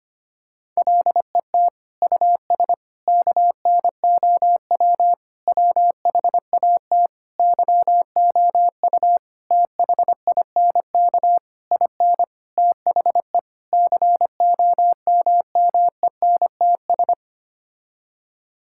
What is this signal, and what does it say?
Morse code, heard as a keyed tone, about 25 wpm, 700 Hz. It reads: LET US KNOW WHAT YOU THINK IN THE COMMENTS
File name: Sets of 9 Words - 25wpm